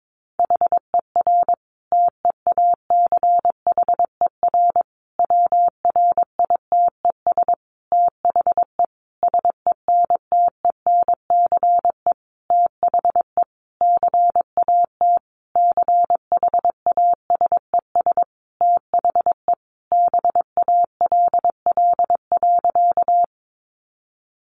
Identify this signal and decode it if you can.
Morse code, heard as a keyed tone, about 22 wpm, 700 Hz. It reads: HER TEACHER WRITES THE SENTENCE THE CAT CHASES THE BALL.
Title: Additional Easy Phrases 22wpm